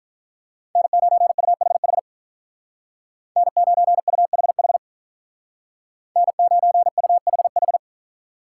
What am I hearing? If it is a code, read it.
Morse code, heard as a keyed tone, about 43 wpm, 700 Hz. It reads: N9UHH N9UHH N9UHH
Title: Speed Racing - USA Call Signs 35wpm